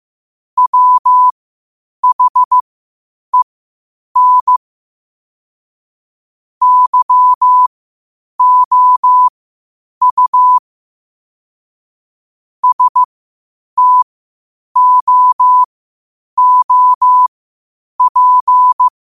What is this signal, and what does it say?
Morse code, heard as a keyed tone, about 15 wpm, spaced as for 8 wpm, 1000 Hz. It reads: WHEN YOU STOOP
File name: qotdfiles / 8wpm